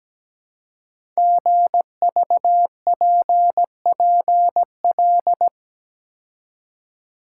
Morse code, heard as a keyed tone, about 17 wpm, 700 Hz. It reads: GVPPL
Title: Code Groups - 5 Letters - 17wpm